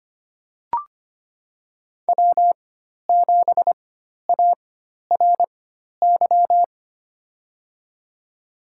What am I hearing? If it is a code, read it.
Morse code, heard as a keyed tone, about 25 wpm, spaced as for 12 wpm, 700 Hz. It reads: W7ARY
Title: USA Call Signs - 5 Characters - 4x Character Spacing - 25wpm